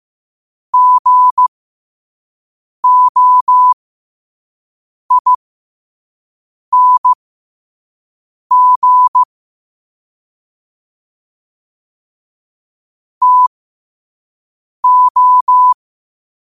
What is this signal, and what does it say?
Morse code, heard as a keyed tone, about 15 wpm, spaced as for 5 wpm, 1000 Hz. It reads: GOING TO
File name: qotdfiles / 5wpm